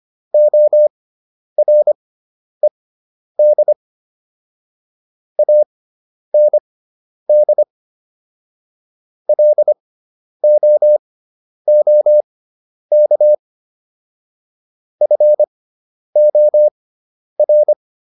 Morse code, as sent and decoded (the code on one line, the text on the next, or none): --- .-. -.. / .- -. -.. / .-.. --- --- -.- / ..-. --- .-.
ORED AND LOOK FOR